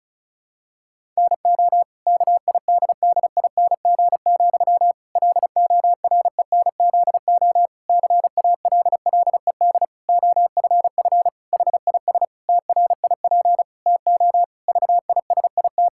Morse code, heard as a keyed tone, about 36 wpm, 700 Hz. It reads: NO KIDDING, LORENZO CALLED OFF HIS TRIP TO VISIT
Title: Pangram Challenge v2 - 35wpm